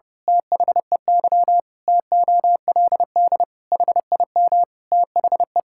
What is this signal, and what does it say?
Morse code, heard as a keyed tone, about 30 wpm, 700 Hz. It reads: THEY TOLD HIM THE
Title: Pangram Challenge v2 - 30wpm